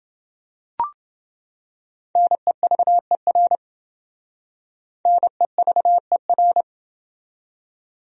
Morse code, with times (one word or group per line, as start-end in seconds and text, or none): 2.2-3.6 NEVER
5.1-6.6 NEVER